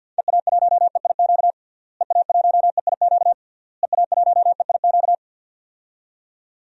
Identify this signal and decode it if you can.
Morse code, heard as a keyed tone, about 51 wpm, 700 Hz. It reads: EA1EIQ EA1EIQ EA1EIQ
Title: DXCC-100 Call Signs - 6 Characters - Repeated 3x - 50wpm